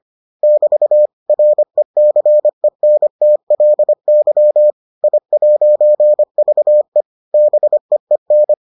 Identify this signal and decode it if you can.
Morse code, heard as a keyed tone, about 25 wpm, 600 Hz. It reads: <BT> RECENTLY I'VE BEEN